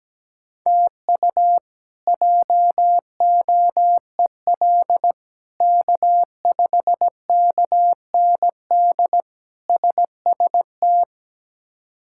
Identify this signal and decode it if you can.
Morse code, heard as a keyed tone, about 17 wpm, 700 Hz. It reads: TU JOEL K5KND SST
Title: K1USN SST QSO Practice - 17wpm